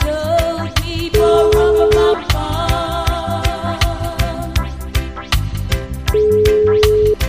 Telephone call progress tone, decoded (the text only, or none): ringback tone